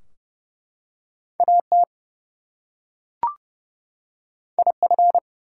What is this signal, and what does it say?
Morse code, heard as a keyed tone, about 30 wpm, 700 Hz. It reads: AT IF
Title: Two Letter Words - 30wpm